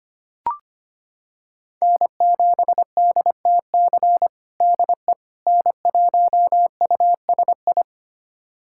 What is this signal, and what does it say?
Morse code, heard as a keyed tone, about 25 wpm, 700 Hz. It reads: N7DTC DE N1USI
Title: USA Callsign DE Callsign - 5 Characters - 25wpm